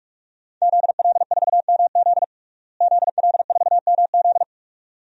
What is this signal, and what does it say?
Morse code, heard as a keyed tone, about 46 wpm, 700 Hz. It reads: ZL4MZ ZL4MZ
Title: DXCC-100 Call Signs - 5 Characters - Repeated 2x - 45wpm